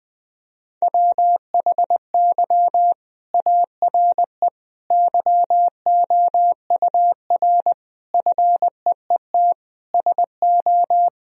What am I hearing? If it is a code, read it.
Morse code, heard as a keyed tone, about 20 wpm, 700 Hz. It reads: WHY ARE YOUR FEET SO
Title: Sentences from Top 1000 Words - Review Part 2 - 20wpm